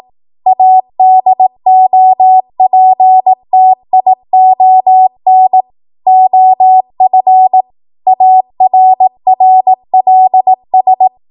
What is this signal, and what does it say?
Morse code, heard as a keyed tone, about 18 wpm, 750 Hz. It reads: ADOPTION OF ARRLS
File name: Morse / 18WPM